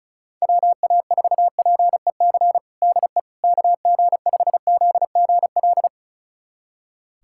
Morse code, heard as a keyed tone, about 34 wpm, 700 Hz. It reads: WA4PEC DE KG5ZGL
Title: USA Callsign DE Callsign - 6 Characters - 35wpm